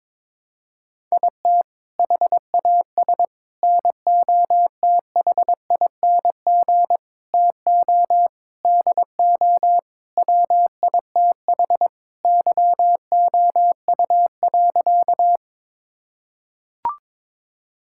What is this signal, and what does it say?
Morse code, heard as a keyed tone, about 22 wpm, 700 Hz. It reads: IT HAS NOTHING TO DO WITH YOU.
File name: practice / 22wpm / Sentences with 4-Word Phrases - 22wpm